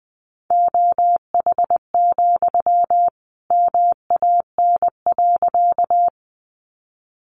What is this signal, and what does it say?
Morse code, heard as a keyed tone, about 20 wpm, 700 Hz. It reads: OH, MAN.
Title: Sentences from Top 500 Words 20wpm